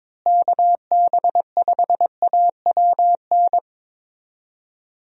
Morse code, heard as a keyed tone, about 22 wpm, 700 Hz. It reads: KB5AWN
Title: USA Call Signs - 6 Characters - Repeated 3x - 22wpm